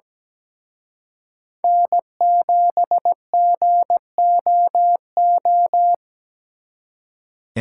Morse code, heard as a keyed tone, about 20 wpm, 700 Hz. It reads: N7GOO
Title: Speed Racing - USA 1x3 Call Signs 17wpm